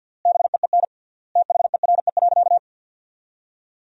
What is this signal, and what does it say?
Morse code, heard as a keyed tone, about 50 wpm, 700 Hz. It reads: BEEN THERE.
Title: 3x Word-Spacing - Sentences from Top 300 Words - 50wpm